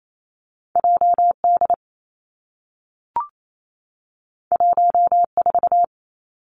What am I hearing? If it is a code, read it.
Morse code, heard as a keyed tone, about 28 wpm, 700 Hz. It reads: J D 1 4
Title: Two Challenging Letter-Number Combinations 28wpm